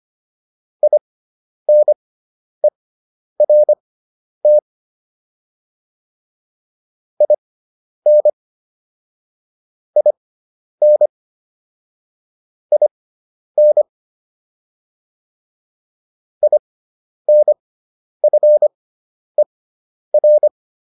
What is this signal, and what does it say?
Morse code, heard as a keyed tone, about 25 wpm, spaced as for 10 wpm, 600 Hz. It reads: INERT IN IN IN INFER